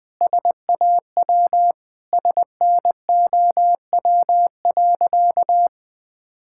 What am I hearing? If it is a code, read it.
Morse code, heard as a keyed tone, about 20 wpm, 700 Hz. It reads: SAW SNOW.
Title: Sentences with 3-Word Phrases - 20wpm